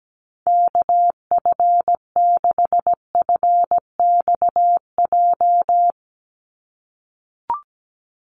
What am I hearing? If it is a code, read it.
Morse code, heard as a keyed tone, about 17 wpm, 700 Hz. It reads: KF6FXJ
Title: USA Call Signs 17wpm